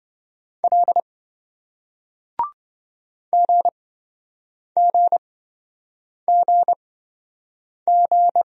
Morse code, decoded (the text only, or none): L GGGG